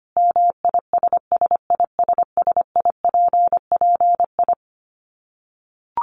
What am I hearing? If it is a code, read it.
Morse code, heard as a keyed tone, about 25 wpm, 700 Hz. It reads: MISSISSIPPI
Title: Dit Challenge 25wpm